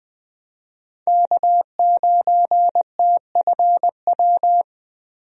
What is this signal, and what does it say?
Morse code, heard as a keyed tone, about 20 wpm, 700 Hz. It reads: K9TFW
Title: USA Call Signs - 5 Characters - 20wpm